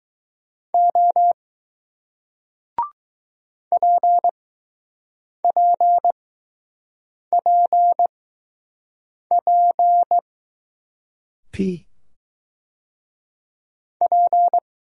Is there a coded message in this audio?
Yes: Morse code from 0.7 to 14.6 s, heard as a keyed tone, about 21 wpm, 700 Hz. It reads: O P P P P P